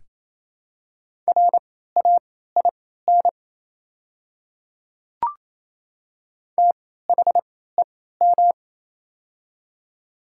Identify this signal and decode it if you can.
Morse code, heard as a keyed tone, about 28 wpm, spaced as for 16 wpm, 700 Hz. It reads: RAIN THEM
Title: 3x Character Spacing - Top 500 Words - 28wpm